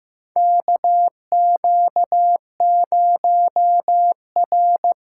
Morse code, heard as a keyed tone, about 15 wpm, 700 Hz. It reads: KQ0R
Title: USA Callsign DE Callsign 15wpm